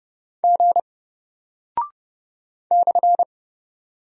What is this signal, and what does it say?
Morse code, heard as a keyed tone, about 30 wpm, 700 Hz. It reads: G /
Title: Single Character (Letter:Number:Punctuation:Prosign) 30wpm